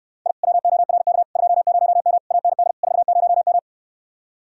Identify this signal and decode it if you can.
Morse code, heard as a keyed tone, about 69 wpm, 700 Hz. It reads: I WORK 20M AND 40M